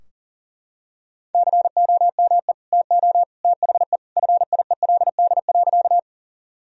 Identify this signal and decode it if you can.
Morse code, heard as a keyed tone, about 40 wpm, 700 Hz. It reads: COME TO THE FIELD.
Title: Sentences from Top 1000 Words - Review Part 2 - 40wpm